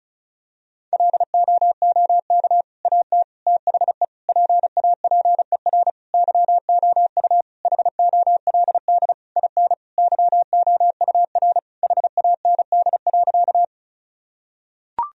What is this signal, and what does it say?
Morse code, heard as a keyed tone, about 34 wpm, 700 Hz. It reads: LOOK AT THE PAPER YOU HOLD IN YOUR HAND.
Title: Sentences from Top 1000 Words - Review Part 2 - 35wpm